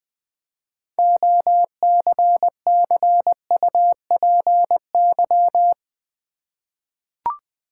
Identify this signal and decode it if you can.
Morse code, heard as a keyed tone, about 20 wpm, 700 Hz. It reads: OCCUPY
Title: 6 Letter Words - First 3 Letters Given - 20wpm